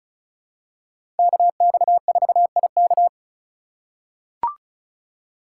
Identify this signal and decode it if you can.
Morse code, heard as a keyed tone, about 34 wpm, 700 Hz. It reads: KX4IK
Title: USA Call Signs - 5 Characters - 6x Character Spacing - 35wpm